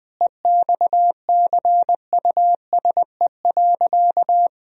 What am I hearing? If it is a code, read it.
Morse code, heard as a keyed tone, about 20 wpm, 700 Hz. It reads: EXCUSE.
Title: Sentences from Top 1000 Words - Review Part 4 - 20wpm